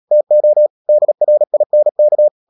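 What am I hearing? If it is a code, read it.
Morse code, heard as a keyed tone, about 37 wpm, 600 Hz. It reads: TO DRINK
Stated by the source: Morse code practice at 35–39 wpm sending English text of Ken Follett's novel Never, from CW med Gnister.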